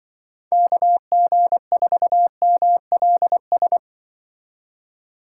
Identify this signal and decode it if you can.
Morse code, heard as a keyed tone, about 24 wpm, 700 Hz. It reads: KG4MLS
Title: Speed Racing - USA Call Signs 20wpm